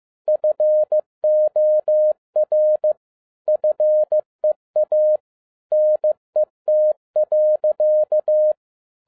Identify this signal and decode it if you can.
Morse code, heard as a keyed tone, about 15 wpm, 610 Hz. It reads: FOR FEA NET.
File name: 2012-01-mp3 / n17-SUMMARY OF FEA NET IN 2011-15wpm